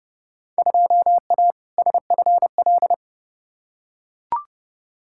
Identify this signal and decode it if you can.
Morse code, heard as a keyed tone, about 30 wpm, 700 Hz. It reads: 2A SFL
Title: Field Day Exchanges Encore - 30wpm